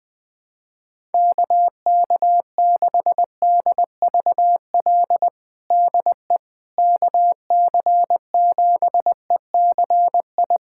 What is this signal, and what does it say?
Morse code, heard as a keyed tone, about 20 wpm, 700 Hz. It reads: KK6DVL DE KC7ECI
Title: USA Callsign DE Callsign - 6 Characters - 20wpm